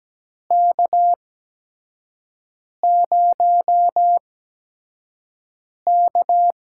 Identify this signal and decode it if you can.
Morse code, heard as a keyed tone, about 17 wpm, spaced as for 5 wpm, 700 Hz. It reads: K0K
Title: USA Call Signs - 4 Characters - 8x Character Spacing - 17wpm